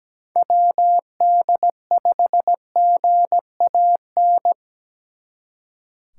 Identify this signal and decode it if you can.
Morse code, heard as a keyed tone, about 20 wpm, 700 Hz. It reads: WD5GAN
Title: Speed Racing - USA 2x3 Call Signs 17wpm